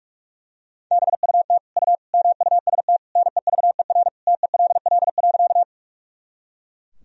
Morse code, heard as a keyed tone, about 41 wpm, 700 Hz. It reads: BUT U MUST NEVER TELL.
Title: Sentences from Top 300 Words and Abbreviations 45wpm